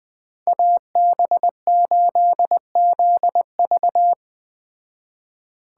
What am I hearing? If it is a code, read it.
Morse code, heard as a keyed tone, about 20 wpm, 700 Hz. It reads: AB8ZV